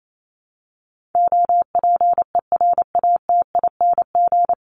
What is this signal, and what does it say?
Morse code, heard as a keyed tone, about 28 wpm, 700 Hz. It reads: OPERATING
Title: ING Suffix - Top 100 Words with Base Words 28wpm